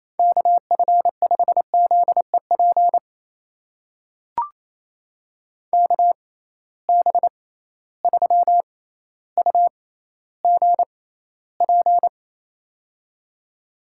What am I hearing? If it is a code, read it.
Morse code, heard as a keyed tone, about 28 wpm, spaced as for 14 wpm, 700 Hz. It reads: KF5ZEP KB3UGP
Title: USA Call Signs - 6 Characters - 6x Character Spacing - 28wpm